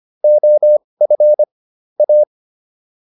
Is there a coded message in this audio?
Yes: Morse code from 0.2 to 2.2 s, heard as a keyed tone, about 25 wpm, spaced as for 20 wpm, 600 Hz. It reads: OF A